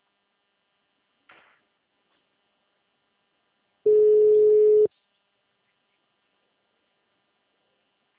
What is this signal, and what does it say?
Ringback tone (Europe, 425 Hz).